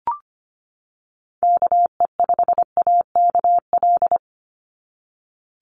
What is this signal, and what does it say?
Morse code, heard as a keyed tone, about 25 wpm, 700 Hz. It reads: KE5AKL
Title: NA SOTA Chaser Callsigns 25wpm